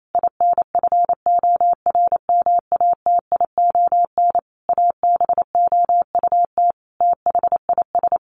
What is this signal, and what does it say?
Morse code, heard as a keyed tone, about 28 wpm, 700 Hz. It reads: INFORMATION ABOUT THIS
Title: Sentences from Top 1200 Words 28wpm